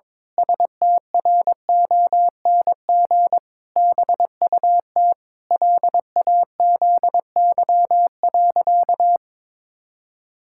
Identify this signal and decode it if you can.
Morse code, heard as a keyed tone, about 21 wpm, 700 Hz. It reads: STRONG BUT LAZY.
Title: Pangram Challenge v2 - 22wpm